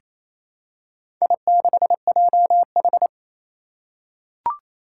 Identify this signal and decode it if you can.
Morse code, heard as a keyed tone, about 28 wpm, 700 Hz. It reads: I6JH
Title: Four Challenging Letter-Number Combinations 28wpm